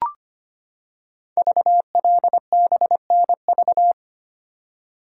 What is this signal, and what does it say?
Morse code, heard as a keyed tone, about 25 wpm, 700 Hz. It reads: VLBNV
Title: Code Groups - 5 Letter:Numbers - 25wpm